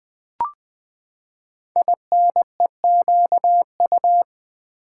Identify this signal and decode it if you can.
Morse code, heard as a keyed tone, about 20 wpm, 700 Hz. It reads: INEQU
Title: Build A Word - 5 Letters At A Time - 10-Letter Words - 20wpm